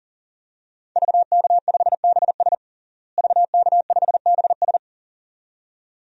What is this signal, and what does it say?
Morse code, heard as a keyed tone, about 37 wpm, 700 Hz. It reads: VK5BS VK5BS
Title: DXCC-100 Call Signs - 5 Characters - Repeated 2x - 40wpm